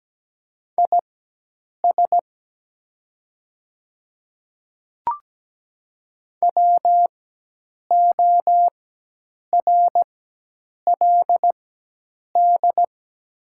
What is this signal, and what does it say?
Morse code, heard as a keyed tone, about 17 wpm, spaced as for 8 wpm, 700 Hz. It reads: IS WORLD
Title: 4x Character Spacing - Top 200 Words - 17wpm